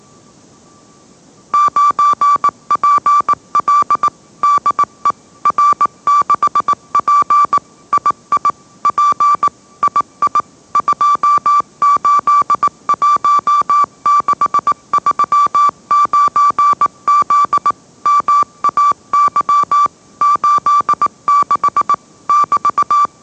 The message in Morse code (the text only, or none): 9PL DE R6P II P II 281639Z MAY 86 <BT>